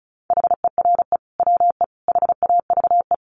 Morse code, heard as a keyed tone, about 34 wpm, 700 Hz. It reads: HERE WE HAVE
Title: Sentences from Top 1100 Words 35wpm